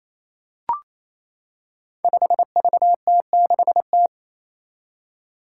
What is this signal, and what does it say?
Morse code, heard as a keyed tone, about 27 wpm, 700 Hz. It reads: 5VT6T
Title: Five Challenging Letter-Number Combinations 28wpm